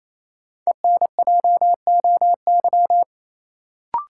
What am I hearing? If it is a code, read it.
Morse code, heard as a keyed tone, about 28 wpm, 700 Hz. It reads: ENJOY
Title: Sentences from Top 1000 Words - Review Part 4 - New Word Drill - 28wpm